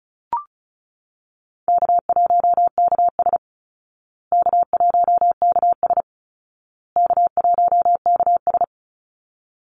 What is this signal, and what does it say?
Morse code, heard as a keyed tone, about 34 wpm, 700 Hz. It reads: K1KS K1KS K1KS
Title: Easy Call Signs Repeated 3x - 35wpm